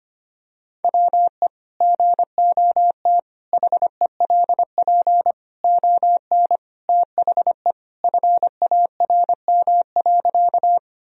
Morse code, heard as a keyed tone, about 25 wpm, 700 Hz. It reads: WE GOT HELP ON THE FARM.
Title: Sentences from Top 1000 Words - Review Part 2 - 25wpm